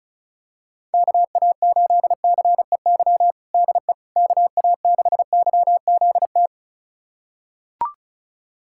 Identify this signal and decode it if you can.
Morse code, heard as a keyed tone, about 36 wpm, 700 Hz. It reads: KA8CEY DE KA6YZT
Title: USA Callsign DE Callsign - 6 Characters - 35wpm